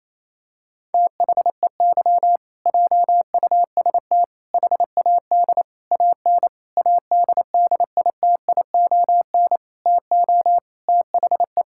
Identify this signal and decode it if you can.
Morse code, heard as a keyed tone, about 28 wpm, 700 Hz. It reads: THEY JUST HAD AN ADDITION TO THE
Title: Sentences from Top 1000 Words - Review Part 6 - 28wpm